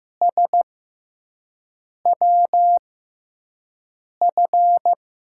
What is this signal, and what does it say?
Morse code, heard as a keyed tone, about 15 wpm, spaced as for 5 wpm, 700 Hz. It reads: SWF
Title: USA Call Signs - 5 Characters - 6x Character Spacing - 15wpm